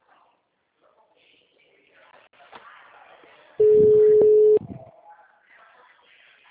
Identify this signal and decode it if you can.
Ringback tone (Europe, 425 Hz).
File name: out- -2016- - - .2